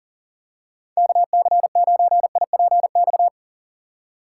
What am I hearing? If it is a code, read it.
Morse code, heard as a keyed tone, about 37 wpm, 700 Hz. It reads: KC9IPX
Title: USA Call Signs - 6 Characters - 6x Character Spacing - 40wpm